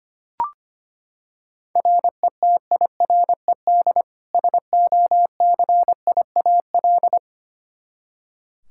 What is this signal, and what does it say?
Morse code, heard as a keyed tone, about 25 wpm, 700 Hz. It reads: RETIRED SOCIAL
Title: K1USN SST QSO Practice 25wpm